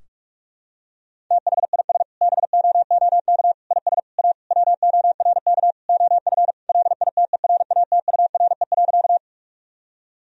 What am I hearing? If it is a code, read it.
Morse code, heard as a keyed tone, about 48 wpm, 700 Hz. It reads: THIS BOOK IS A WORK OF LITERATURE.
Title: Sentences from Top 1000 Words - Review Part 3 - 45wpm